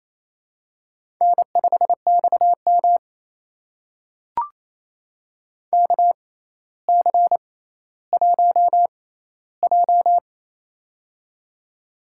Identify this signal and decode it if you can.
Morse code, heard as a keyed tone, about 28 wpm, spaced as for 14 wpm, 700 Hz. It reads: N5XM KC1J
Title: USA Call Signs - 4 Characters - 6x Character Spacing - 28wpm